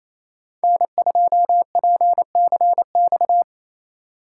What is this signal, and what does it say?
Morse code, heard as a keyed tone, about 34 wpm, 700 Hz. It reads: N2PCX
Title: Speed Racing - USA 1x3 Call Signs 28wpm